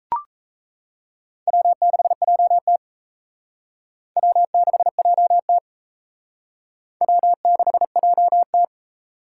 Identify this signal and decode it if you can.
Morse code, heard as a keyed tone, about 35 wpm, 700 Hz. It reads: W6JT W6JT W6JT
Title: Speed Racing - USA 1x2 Call Signs 28wpm